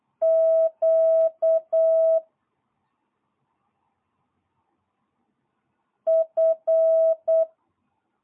Morse code, heard as a keyed tone, about 8 wpm, 1300 Hz, with noise. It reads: Q F
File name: jh8cbh / 2022cw